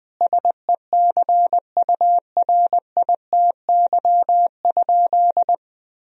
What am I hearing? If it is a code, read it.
Morse code, heard as a keyed tone, about 20 wpm, 700 Hz. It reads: SECURITY?
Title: Sentences from Top 1000 Words - Review Part 4 - 20wpm